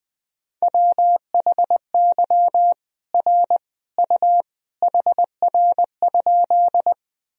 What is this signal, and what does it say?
Morse code, heard as a keyed tone, about 20 wpm, 700 Hz. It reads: WHY R U HR?
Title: Sentences from Top 200 Words and Abbreviations 20wpm